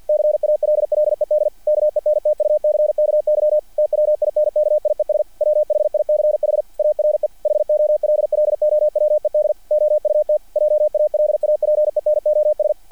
Morse code, heard as a keyed tone, about 48 wpm, 600 Hz. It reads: CALLED KENTUCKY TWINKIES WHICH ARE HOLLOWED OUT JALAPENOS
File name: archived / CRSnet